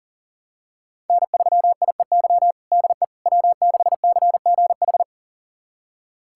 Morse code, heard as a keyed tone, about 40 wpm, 700 Hz. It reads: N3IEY DE W6CGH
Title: USA Callsign DE Callsign - 5 Characters - 40wpm